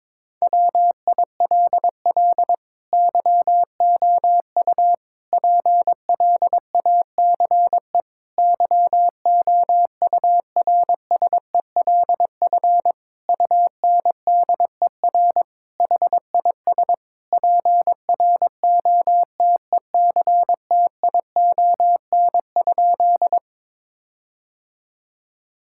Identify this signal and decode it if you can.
Morse code, heard as a keyed tone, about 22 wpm, 700 Hz. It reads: WILL YOU PLACE YOURSELF UNDER HIS PROTECTION?
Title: Sentences from Top 1600 Words 22wpm